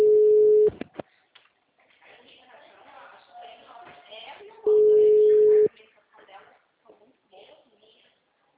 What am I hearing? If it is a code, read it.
Ringback tone (Europe, 425 Hz), with speech.